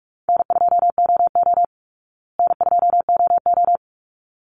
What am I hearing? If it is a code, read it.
Morse code, heard as a keyed tone, about 44 wpm, 700 Hz. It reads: N2OO N2OO
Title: Easy Call Signs Repeated 2x - 45wpm